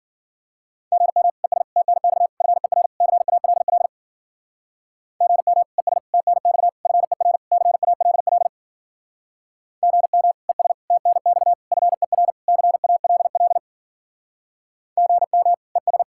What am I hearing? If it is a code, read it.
Morse code, heard as a keyed tone, about 49 wpm, 700 Hz. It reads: GM ES TNX FER CALL GM ES TNX FER CALL GM ES TNX FER CALL GM ES
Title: Speed-Racing Common QSO Phrases - 40wpm